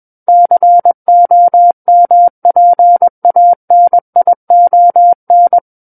Morse code, heard as a keyed tone, about 21 wpm, 700 Hz. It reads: COMPANION
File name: Book One / 12-Chapter 12 - 21 WPM